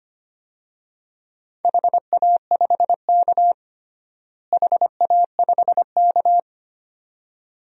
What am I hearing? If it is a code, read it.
Morse code, heard as a keyed tone, about 25 wpm, 700 Hz. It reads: HA5K HA5K
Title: DXCC-100 Call Signs - 4 Characters - Repeated 2x - 25wpm